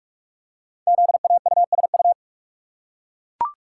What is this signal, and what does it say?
Morse code, heard as a keyed tone, about 44 wpm, 690 Hz. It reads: ZAUSU
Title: Code Groups - 5 Letters - 45wpm